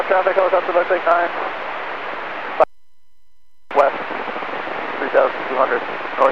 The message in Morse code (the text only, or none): none